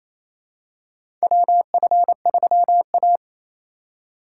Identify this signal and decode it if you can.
Morse code, heard as a keyed tone, about 28 wpm, 700 Hz. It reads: WF3A
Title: USA Call Signs - 4 Characters - 4x Character Spacing - 28wpm